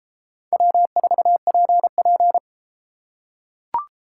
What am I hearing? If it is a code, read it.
Morse code, heard as a keyed tone, about 32 wpm, 700 Hz. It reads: W4PP